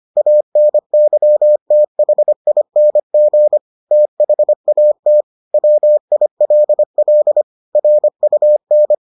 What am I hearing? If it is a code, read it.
Morse code, heard as a keyed tone, about 25 wpm, 600 Hz. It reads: ANYTHING THAT WILL RUN